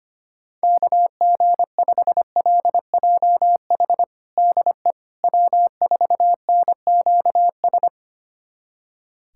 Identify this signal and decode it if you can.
Morse code, heard as a keyed tone, about 25 wpm, 700 Hz. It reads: KG5LJH DE W4NQS
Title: USA Callsign DE Callsign 25wpm